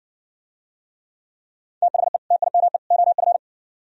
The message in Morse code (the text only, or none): THE TIME OF